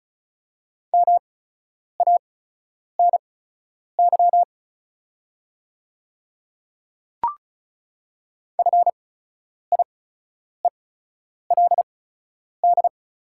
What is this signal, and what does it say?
Morse code, heard as a keyed tone, about 36 wpm, spaced as for 10 wpm, 690 Hz. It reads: MANY FIELD
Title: 8x Character Spacing - Top 500 Words - 35wpm